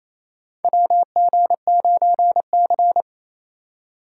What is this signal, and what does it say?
Morse code, heard as a keyed tone, about 28 wpm, 700 Hz. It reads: WG9C
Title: USA Call Signs - 4 Characters - 2x Character Spacing - 28wpm